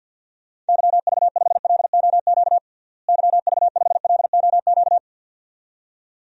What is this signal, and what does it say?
Morse code, heard as a keyed tone, about 53 wpm, 700 Hz. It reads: YV5LOQ YV5LOQ
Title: DXCC-100 Call Signs - 6 Characters - Repeated 2x - 50wpm